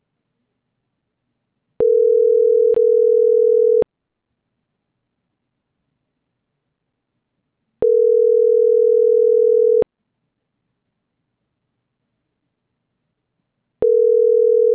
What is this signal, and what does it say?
Ringback tone (North America), 440 + 480 Hz.